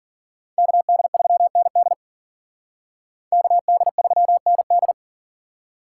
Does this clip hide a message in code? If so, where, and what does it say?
Yes: Morse code from 0.6 to 4.9 s, heard as a keyed tone, about 51 wpm, 700 Hz. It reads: KD3ND KD3ND